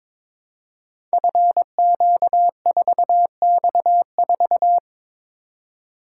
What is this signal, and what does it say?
Morse code, heard as a keyed tone, about 22 wpm, 700 Hz. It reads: FQ4X4
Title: Code Groups - 5 Letter:Numbers - 22wpm